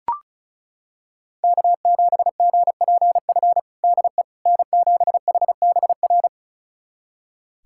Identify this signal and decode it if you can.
Morse code, heard as a keyed tone, about 36 wpm, 700 Hz. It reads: K7GPF DE N7HBR
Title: USA Callsign DE Callsign - 5 Characters - 35wpm